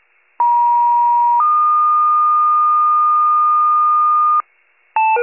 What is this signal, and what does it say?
Two-tone page, heard as a one-second tone, then a three-second tone, about 950 then 1220 Hz, with noise. It is heard from 0.4 to 4.4 s.
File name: wcalert